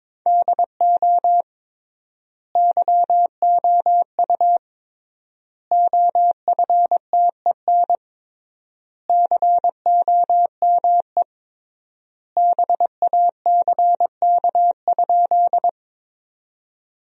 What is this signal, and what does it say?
Morse code, heard as a keyed tone, about 22 wpm, 700 Hz. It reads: DO YOU OFTEN COME BACK?
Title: 3x Word-Spacing - Sentences from Top 300 Words - 22wpm